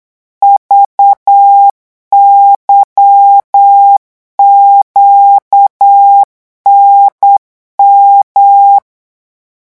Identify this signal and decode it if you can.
Morse code, heard as a keyed tone, about 8 wpm, 800 Hz. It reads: VYQNM
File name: morse mp3 / revision3-vitesse 8 mots